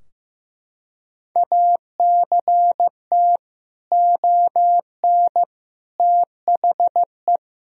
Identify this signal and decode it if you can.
Morse code, heard as a keyed tone, about 15 wpm, 700 Hz. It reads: ACT ON THE
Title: Sentences from Top 1000 Words - Review Part 2 - 15wpm